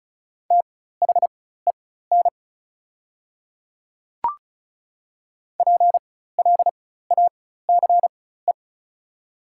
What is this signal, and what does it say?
Morse code, heard as a keyed tone, about 36 wpm, spaced as for 16 wpm, 690 Hz. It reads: THEN PLACE
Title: 4x Character Spacing - Top 200 Words - 35wpm